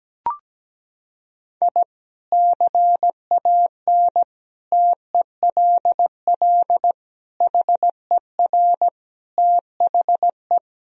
Morse code, heard as a keyed tone, about 17 wpm, 700 Hz. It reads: I CAN TELL HER THE
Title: Sentences from Top 1000 Words - Review Part 3 - 17wpm